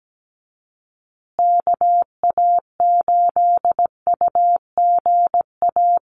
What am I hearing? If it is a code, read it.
Morse code, heard as a keyed tone, about 17 wpm, 700 Hz. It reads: KA8UGA
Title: USA Call Signs Repeated 2x 17wpm